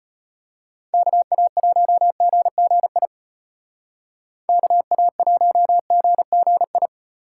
Morse code, heard as a keyed tone, about 30 wpm, 700 Hz. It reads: KA1GGI KA1GGI
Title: Speed Racing - USA 2x3 Call Signs 25wpm